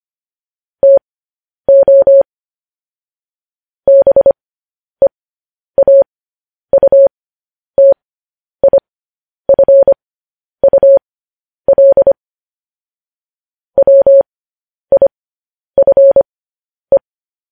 Morse code, heard as a keyed tone, about 25 wpm, spaced as for 10 wpm, 550 Hz. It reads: TO BEAUTIFUL WIFE
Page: News Headlines in Morse Code at 10 WPM